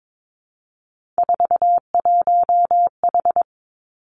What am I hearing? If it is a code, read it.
Morse code, heard as a keyed tone, about 22 wpm, 700 Hz. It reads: 41H
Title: Three Challenging Letter-Number Combinations 22wpm